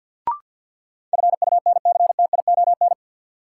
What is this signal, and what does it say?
Morse code, heard as a keyed tone, about 50 wpm, 700 Hz. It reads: FUNCTION
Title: Sentences from Top 1100 Words - New Word Drill - 50wpm